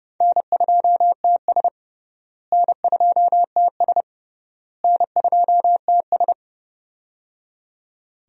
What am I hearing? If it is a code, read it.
Morse code, heard as a keyed tone, about 30 wpm, 700 Hz. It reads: N2TS N2TS N2TS
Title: USA Call Signs - 4 Characters - Repeated 3x - 30wpm